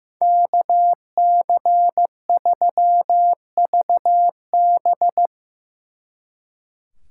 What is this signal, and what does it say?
Morse code, heard as a keyed tone, about 17 wpm, 700 Hz. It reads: KC3VB
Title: Speed Racing - USA 2x2 Call Signs 15wpm